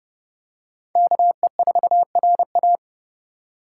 Morse code, heard as a keyed tone, about 32 wpm, 700 Hz. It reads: KE4RA
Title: USA Call Signs - 4x Character Spacing - 30wpm